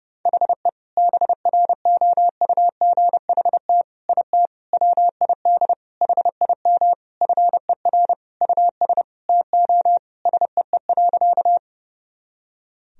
Morse code, heard as a keyed tone, about 30 wpm, 700 Hz. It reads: HE BROUGHT IT WID HIM FER US TO SEE.
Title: Sentences from Top 500 Words and Abbreviations 30wpm